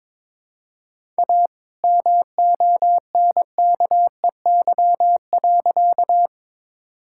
Morse code, heard as a keyed tone, about 22 wpm, 700 Hz. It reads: A MONKEY.
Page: morse-code-ninja-practice-22wpm